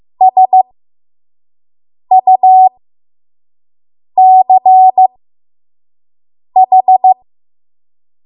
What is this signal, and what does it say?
Morse code, heard as a keyed tone, about 15 wpm, spaced as for 5 wpm, 750 Hz. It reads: SUCH
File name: Morse / 05WPM